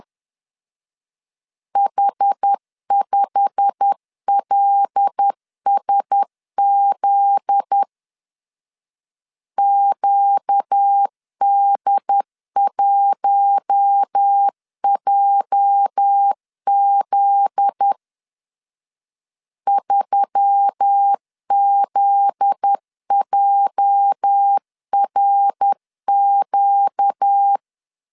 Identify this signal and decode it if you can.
Morse code, heard as a keyed tone, about 10 wpm, 800 Hz. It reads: H5LSZ QD1JZ 3ZJRQ
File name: LEZIONI / 8 YJ45 / 8 YJ45 10wpm 2549